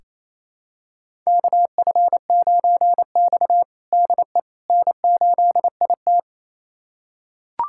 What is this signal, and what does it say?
Morse code, heard as a keyed tone, about 28 wpm, 700 Hz. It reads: KF9X DE N8IT
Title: USA Callsign DE Callsign - 4 Characters - 28wpm